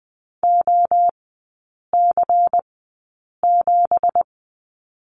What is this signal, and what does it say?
Morse code, heard as a keyed tone, about 20 wpm, spaced as for 8 wpm, 700 Hz. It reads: OC7
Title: 166 Lesson 27e - Number 7 - Three Characters 20wpm